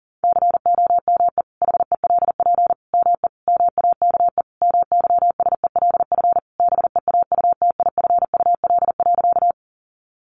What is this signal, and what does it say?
Morse code, heard as a keyed tone, about 40 wpm, 700 Hz. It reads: COME HELP ME MAKE MYSELF BEAUTIFUL.
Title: Sentences from Top 1300 Words 40wpm